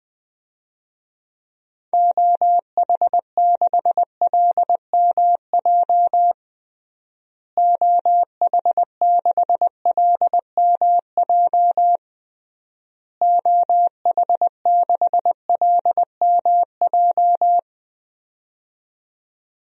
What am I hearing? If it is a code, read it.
Morse code, heard as a keyed tone, about 20 wpm, 700 Hz. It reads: OH6LMJ OH6LMJ OH6LMJ
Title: DXCC-100 Call Signs - 6 Characters - Repeated 3x - 20wpm